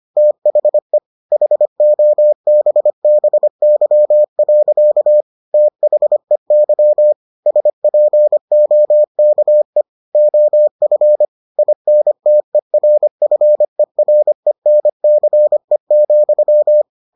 Morse code, heard as a keyed tone, about 25 wpm, 600 Hz. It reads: THE HOBBY. THEY SPOKE OF INTERFERENCE,